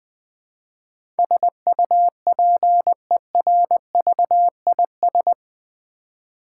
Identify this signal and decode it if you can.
Morse code, heard as a keyed tone, about 20 wpm, 700 Hz. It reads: SUPERVIS
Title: Build A Word - 4 Letters At A Time - 10-Letter Words - 20wpm